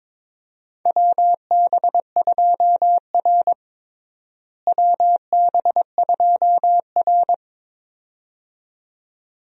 Morse code, heard as a keyed tone, about 21 wpm, 700 Hz. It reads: WB2R WB2R
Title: USA Call Signs - 4 Characters - Repeated 2x - 22wpm